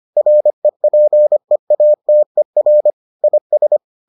Morse code, heard as a keyed tone, about 25 wpm, 600 Hz. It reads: REPEATER IS